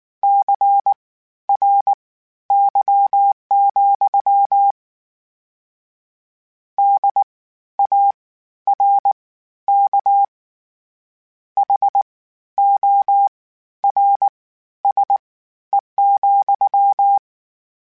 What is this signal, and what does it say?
Morse code, heard as a keyed tone, about 19 wpm, spaced as for 12 wpm, 800 Hz. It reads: CRY, DARK HORSE,
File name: Misc / Misc-mp3s / WordPairs sp @18